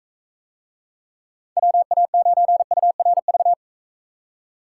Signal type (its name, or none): Morse code